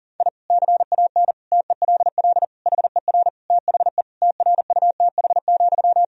Morse code, heard as a keyed tone, about 40 wpm, 700 Hz. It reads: I CAN TELL HER THE TRUTH,
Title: Sentences from Top 1000 Words - Review Part 3 - 40wpm